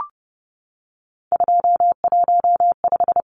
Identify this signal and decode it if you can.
Morse code, heard as a keyed tone, about 30 wpm, 700 Hz. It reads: 215
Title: Numbers - 3-Digit 30wpm